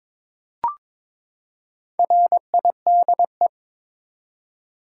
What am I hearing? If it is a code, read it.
Morse code, heard as a keyed tone, about 22 wpm, 700 Hz. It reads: RIDE
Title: Four Letter Words - 22wpm